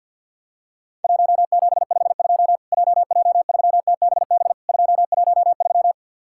Morse code, heard as a keyed tone, about 50 wpm, 700 Hz. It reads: 1752 JJ3TBB 213
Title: Worked All Europe DX Contest - QTC Exchange - 50wpm